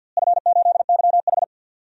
Morse code, heard as a keyed tone, about 49 wpm, 700 Hz. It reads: F8YH
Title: DXCC-100 Call Signs - 4 Characters - Repeated 2x - 50wpm